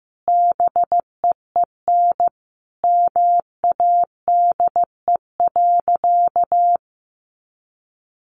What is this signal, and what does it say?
Morse code, heard as a keyed tone, about 15 wpm, 700 Hz. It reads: BEEN MADE.
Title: Sentences from Top 1400 Words 15wpm